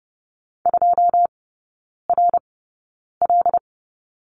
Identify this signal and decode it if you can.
Morse code, heard as a keyed tone, about 30 wpm, spaced as for 9 wpm, 700 Hz. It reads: 2RL
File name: 081 Lesson 14e - Number 2 - Three Characters 30wpm